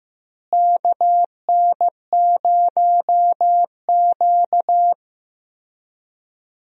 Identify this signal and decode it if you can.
Morse code, heard as a keyed tone, about 16 wpm, 700 Hz. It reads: KN0Q
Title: USA Call Signs - 4 Characters - Repeated 3x - 15wpm